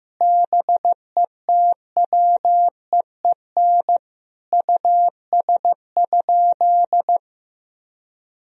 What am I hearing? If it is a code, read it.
Morse code, heard as a keyed tone, about 15 wpm, 700 Hz. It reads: BETWEEN US?
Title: Sentences from Top 1000 Words - Review Part 2 - 15wpm